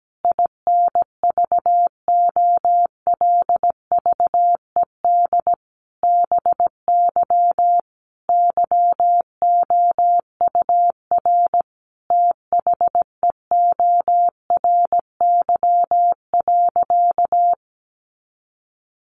Morse code, heard as a keyed tone, about 17 wpm, 700 Hz. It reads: INVOLVED BY YOUR THEORY.
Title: Sentences from Top 800 Words 17wpm